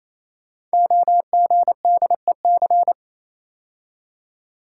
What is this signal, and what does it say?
Morse code, heard as a keyed tone, about 27 wpm, 700 Hz. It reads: OGDEC
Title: Code Groups - 5 Letters - 28wpm